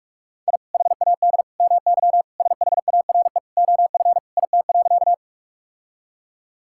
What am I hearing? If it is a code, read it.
Morse code, heard as a keyed tone, about 46 wpm, 700 Hz. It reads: I HAD MY SHARE OF IT.
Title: Sentences from Top 1000 Words - Review Part 6 - 45wpm